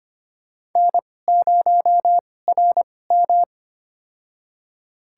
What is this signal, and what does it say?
Morse code, heard as a keyed tone, about 25 wpm, spaced as for 18 wpm, 700 Hz. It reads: N0RM
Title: USA Call Signs - 4 Characters - 2x Character Spacing - 25wpm